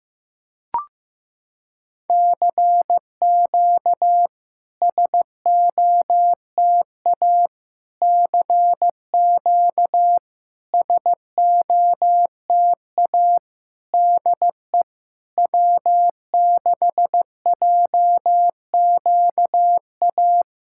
Morse code, heard as a keyed tone, about 15 wpm, 700 Hz. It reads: CQ SOTA CQ SOTA DE W6JQA
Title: SOTA QSO Practice 15wpm